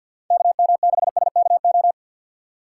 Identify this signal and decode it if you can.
Morse code, heard as a keyed tone, about 47 wpm, 700 Hz. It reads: KM6SKO
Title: USA Call Signs - 6 Characters - 6x Character Spacing - 50wpm